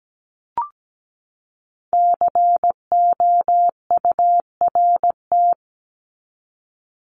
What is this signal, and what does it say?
Morse code, heard as a keyed tone, about 17 wpm, 700 Hz. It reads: COURT
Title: Top 1501-1750 Words 17wpm